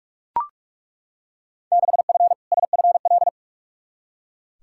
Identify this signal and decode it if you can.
Morse code, heard as a keyed tone, about 42 wpm, 700 Hz. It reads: 6F SFL
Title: Field Day Exchanges Encore - 45wpm